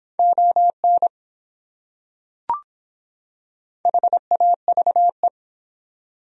Morse code, heard as a keyed tone, about 25 wpm, 700 Hz. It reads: ON HAVE